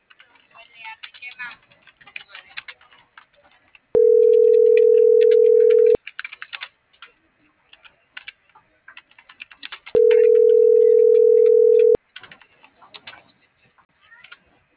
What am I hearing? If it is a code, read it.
Ringback tone (North America), 440 + 480 Hz.